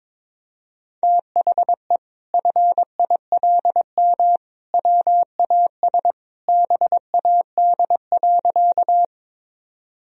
Morse code, heard as a keyed tone, about 21 wpm, 700 Hz. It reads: THE FILM WAS BAD.
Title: Sentences from Top 1000 Words - Review Part 3 - 22wpm